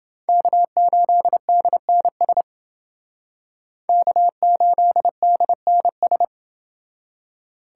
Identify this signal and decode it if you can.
Morse code, heard as a keyed tone, about 29 wpm, 700 Hz. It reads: K8DNS K8DNS